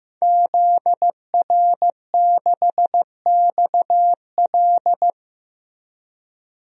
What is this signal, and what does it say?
Morse code, heard as a keyed tone, about 15 wpm, 700 Hz. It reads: ZR6XL
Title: DXCC-100 Call Signs - 5 Characters - Repeated 2x - 15wpm